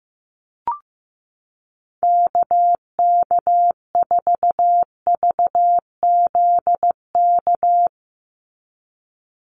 Morse code, heard as a keyed tone, about 15 wpm, 700 Hz. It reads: KK4VZK
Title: USA Call Signs 15wpm